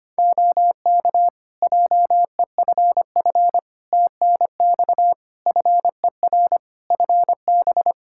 Morse code, heard as a keyed tone, about 25 wpm, 700 Hz. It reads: OK JEFF TNX FER FB
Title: Practice QSO - Exchange 4 - Ending - 25wpm